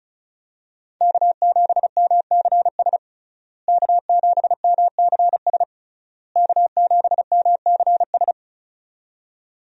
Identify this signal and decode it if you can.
Morse code, heard as a keyed tone, about 36 wpm, 700 Hz. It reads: K7MCS K7MCS K7MCS
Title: USA Call Signs - 5 Characters - Repeated 3x - 35wpm